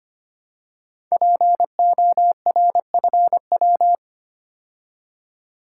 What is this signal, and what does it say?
Morse code, heard as a keyed tone, about 25 wpm, 700 Hz. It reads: PORFW
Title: Code Groups - 5 Letters - 25wpm